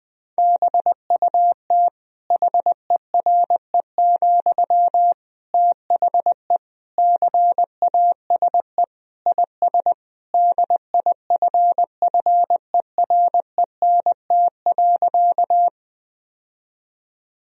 Morse code, heard as a keyed tone, about 21 wpm, 700 Hz. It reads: BUT HERE, THE CASE IS DIFFERENT.
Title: Sentences from Top 1000 Words - Review Part 3 - 20wpm